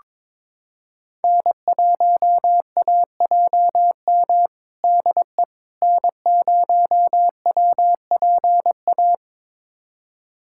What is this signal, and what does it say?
Morse code, heard as a keyed tone, about 21 wpm, 700 Hz. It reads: N1AJM DE N0WPA
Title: USA Callsign DE Callsign 22wpm